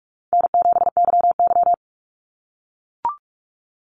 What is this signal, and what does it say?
Morse code, heard as a keyed tone, about 44 wpm, 700 Hz. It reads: N7YY
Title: NA SOTA Chaser Callsigns 45wpm